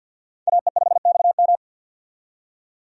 Morse code, heard as a keyed tone, about 52 wpm, 700 Hz. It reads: AE5XM
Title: USA Call Signs - 5 Characters - 50wpm